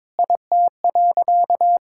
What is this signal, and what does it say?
Morse code, heard as a keyed tone, about 22 wpm, 700 Hz. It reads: IT.
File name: practice / 22wpm / Sentences with 4-Word Phrases - 22wpm